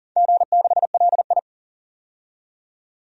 Morse code, heard as a keyed tone, about 37 wpm, 700 Hz. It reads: G6LI
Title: DXCC-100 Call Signs - 4 Characters - 40wpm